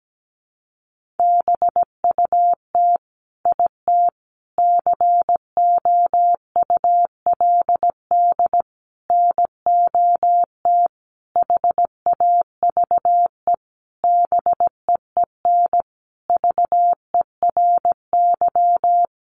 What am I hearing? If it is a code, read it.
Morse code, heard as a keyed tone, about 17 wpm, 700 Hz. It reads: BUT IT COULD NOT HAVE BEEN VERY
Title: Sentences from Top 1400 Words 17wpm